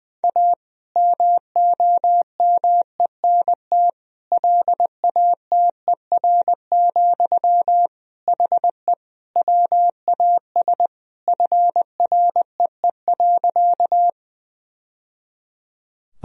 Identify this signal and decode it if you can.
Morse code, heard as a keyed tone, about 20 wpm, 700 Hz. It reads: A MOMENT LATER, HE WAS FREE.
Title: Sentences from Top 1000 Words - Review Part 4 - 20wpm